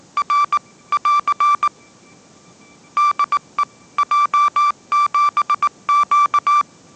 Morse code, heard as a keyed tone, about 19 wpm, 1170 Hz. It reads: R <AR> DE J7Q
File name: 20wpm